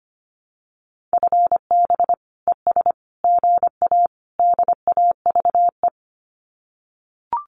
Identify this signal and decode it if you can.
Morse code, heard as a keyed tone, about 25 wpm, 700 Hz. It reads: FB ES GA DAVE
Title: Common QSO Phrases Encore 25wpm